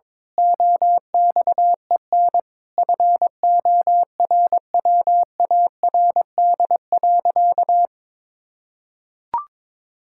Morse code, heard as a keyed tone, about 22 wpm, 700 Hz. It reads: OXEN FORWARD.
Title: Pangram Challenge v2 - 22wpm